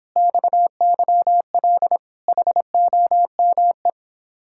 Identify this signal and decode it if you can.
Morse code, heard as a keyed tone, about 26 wpm, 700 Hz. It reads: XYL HOME
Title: Speed-Racing Common QSO Phrases Encore - 22wpm